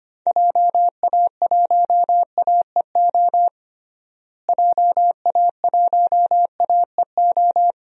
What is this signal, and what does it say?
Morse code, heard as a keyed tone, about 25 wpm, 700 Hz. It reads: JA1AEO JA1AEO
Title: DXCC-100 Call Signs - 6 Characters - Repeated 2x - 25wpm